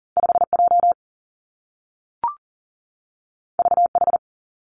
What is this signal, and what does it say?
Morse code, heard as a keyed tone, about 42 wpm, 700 Hz. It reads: 5 J V H
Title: Two Challenging Letter-Number Combinations 40wpm